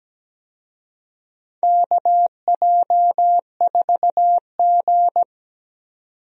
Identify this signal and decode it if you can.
Morse code, heard as a keyed tone, about 17 wpm, 700 Hz. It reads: KJ4G
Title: USA Call Signs - 4 Characters - 17wpm